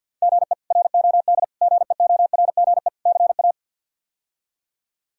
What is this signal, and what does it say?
Morse code, heard as a keyed tone, about 52 wpm, 690 Hz. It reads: GE ROD GEORGE CA
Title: K1USN SST QSO Practice - 50wpm